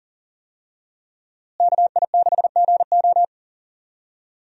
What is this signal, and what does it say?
Morse code, heard as a keyed tone, about 39 wpm, 700 Hz. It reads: KI6GO
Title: USA Call Signs - 5 Characters - Repeated 2x - 40wpm